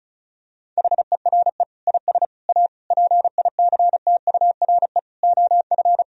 Morse code, heard as a keyed tone, about 34 wpm, 700 Hz. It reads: HERE IS A PICTURE OF
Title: Sentences from Top 1000 Words - Review Part 1 - 35wpm